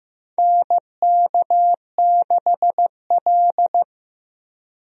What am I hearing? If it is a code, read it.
Morse code, heard as a keyed tone, about 15 wpm, 700 Hz. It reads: NK6L
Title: USA Call Signs - 4 Characters - Repeated 2x - 15wpm